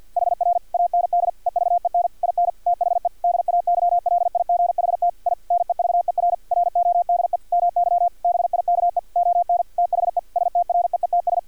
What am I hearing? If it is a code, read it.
Morse code, heard as a keyed tone, about 50 wpm, 700 Hz. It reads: HR AND EVEN IN THE DAYLIGHT I NEVER RODE MY BIKE ON THE STREETS
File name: archived / CRSnet